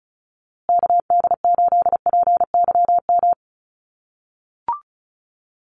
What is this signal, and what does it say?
Morse code, heard as a keyed tone, about 34 wpm, 700 Hz. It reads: KD8PYM
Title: USA Call Signs Repeated 2x 35wpm